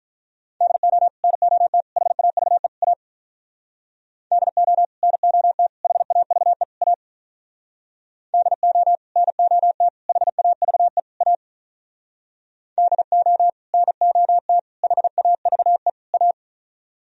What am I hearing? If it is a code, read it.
Morse code, heard as a keyed tone, about 47 wpm, 700 Hz. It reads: DO NOT HAVE A DO NOT HAVE A DO NOT HAVE A DO NOT HAVE A